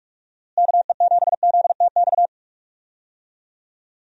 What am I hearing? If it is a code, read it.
Morse code, heard as a keyed tone, about 41 wpm, 700 Hz. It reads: KE7ZTX
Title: USA Call Signs - 6 Characters - Repeated 3x - 45wpm